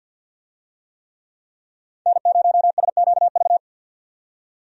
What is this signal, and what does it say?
Morse code, heard as a keyed tone, about 52 wpm, 700 Hz. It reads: N0SQV
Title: USA Call Signs - 5 Characters - 50wpm